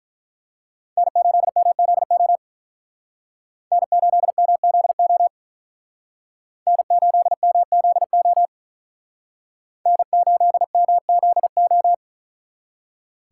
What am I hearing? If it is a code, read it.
Morse code, heard as a keyed tone, about 43 wpm, 700 Hz. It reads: N8MZO N8MZO N8MZO N8MZO
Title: Speed Racing - USA 1x3 Call Signs 35wpm